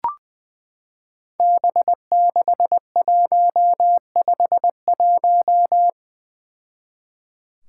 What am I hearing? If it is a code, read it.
Morse code, heard as a keyed tone, about 20 wpm, 700 Hz. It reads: B6151
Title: Five Challenging Letter-Number Combinations 20wpm